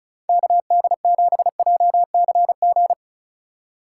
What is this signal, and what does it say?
Morse code, heard as a keyed tone, about 34 wpm, 700 Hz. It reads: KD7JCG
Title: USA Call Signs - 6 Characters - 3x Character Spacing - 35wpm